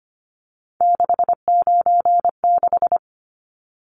Morse code, heard as a keyed tone, about 25 wpm, 690 Hz. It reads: 696
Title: Numbers - 3-Digit 25wpm